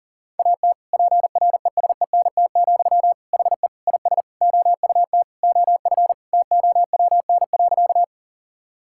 Morse code, heard as a keyed tone, about 37 wpm, 700 Hz. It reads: AT PRESENT, HE IS OUT OF TOWN.
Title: Sentences from Top 1000 Words - Review Part 5 - 40wpm